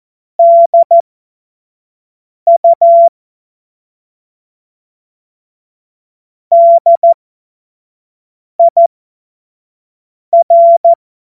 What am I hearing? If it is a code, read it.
Morse code, heard as a keyed tone, about 14 wpm, spaced as for 5 wpm, 680 Hz. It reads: DU DIR
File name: Morsekurs / Witze / Witz 20. Dezember 2022 25 F70